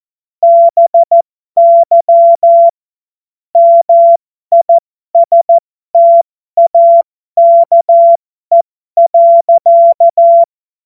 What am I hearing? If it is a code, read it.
Morse code, heard as a keyed tone, about 14 wpm, spaced as for 12 wpm, 680 Hz. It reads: BY MISTAKE.
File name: Morsekurs / Witze